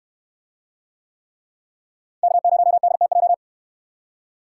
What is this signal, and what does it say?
Morse code, heard as a keyed tone, about 63 wpm, 700 Hz. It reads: K0DTJ